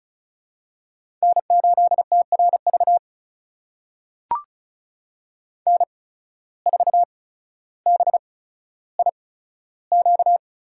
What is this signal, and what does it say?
Morse code, heard as a keyed tone, about 36 wpm, spaced as for 15 wpm, 700 Hz. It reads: N8TRV N4BIQ
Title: USA Call Signs - 5 Characters - 8x Character Spacing - 35wpm